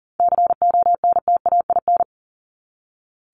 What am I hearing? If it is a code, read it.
Morse code, heard as a keyed tone, about 40 wpm, 700 Hz. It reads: CONTAIN
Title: ING Suffix - Top 100 Words with Base Words 40wpm